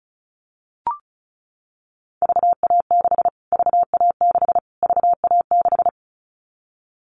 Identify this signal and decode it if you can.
Morse code, heard as a keyed tone, about 34 wpm, 700 Hz. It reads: VA6 VA6 VA6
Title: Callsign Prefixes for Top 25 DXCC - Repeated 3x - 35wpm